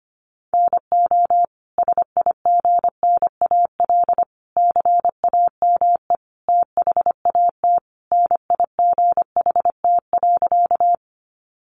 Morse code, heard as a keyed tone, about 25 wpm, 700 Hz. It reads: NO SIGNAL CAME THAT NIGHT.
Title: Sentences from Top 1400 Words 25wpm